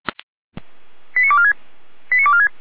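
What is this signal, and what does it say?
Five-tone selective call (ZVEI), heard as quick runs of five tones, tones of about 70 ms. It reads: 89246 89246
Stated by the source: ZVEI five-tone selective call 89167, received by SDR on the 455 channel, fire brigade Feuerwehr Orlamünde.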